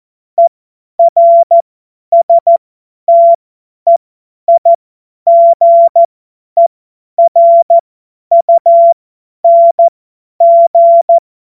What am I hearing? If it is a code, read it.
Morse code, heard as a keyed tone, about 14 wpm, spaced as for 10 wpm, 680 Hz. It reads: ERSTEIGERUNG
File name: Morsekurs / Witze / Witz 14. März 2023 50 F70